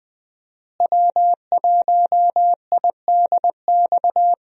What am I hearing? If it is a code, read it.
Morse code, heard as a keyed tone, about 20 wpm, 700 Hz. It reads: W1IDX
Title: USA Call Signs - 5 Characters - Repeated 2x - 20wpm